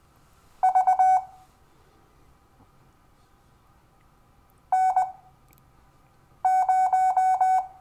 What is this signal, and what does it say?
Morse code, heard as a keyed tone, about 20 wpm, spaced as for 6 wpm, 750 Hz. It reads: V N0
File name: cw / Lesson12 YV